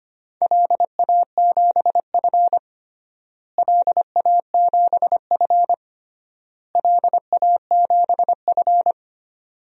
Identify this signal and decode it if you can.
Morse code, heard as a keyed tone, about 25 wpm, 700 Hz. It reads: LA7F LA7F LA7F
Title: DXCC-100 Call Signs - 4 Characters - Repeated 3x - 25wpm